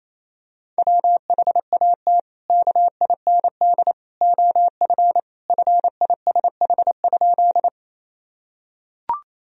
Morse code, heard as a keyed tone, about 28 wpm, 700 Hz. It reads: WHAT KIND OF FISH?
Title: Sentences from Top 1000 Words - Review Part 1 - 28wpm